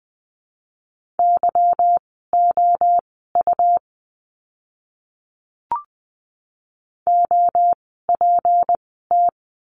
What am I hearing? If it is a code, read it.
Morse code, heard as a keyed tone, about 20 wpm, spaced as for 14 wpm, 700 Hz. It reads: YOU OPT
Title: 153 Lesson 25f - Letter p - Words 20wpm